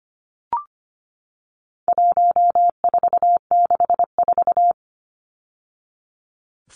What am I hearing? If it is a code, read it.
Morse code, heard as a keyed tone, about 25 wpm, 700 Hz. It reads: 1464
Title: Numbers - 4-Digit 25wpm